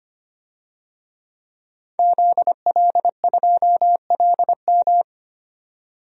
Morse code, heard as a keyed tone, about 25 wpm, 700 Hz. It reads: ZL2LM
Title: DXCC-100 Call Signs - 5 Characters - 25wpm